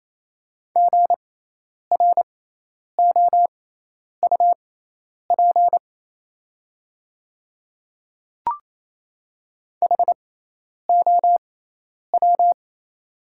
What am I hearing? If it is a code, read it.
Morse code, heard as a keyed tone, about 28 wpm, spaced as for 10 wpm, 700 Hz. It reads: GROUP HOW
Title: 6x Character Spacing - Top 300 Words - 28wpm